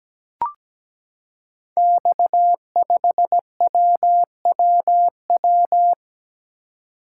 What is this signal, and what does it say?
Morse code, heard as a keyed tone, about 17 wpm, 700 Hz. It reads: X5WWW
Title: Code Groups - 5 Letter:Numbers - 17wpm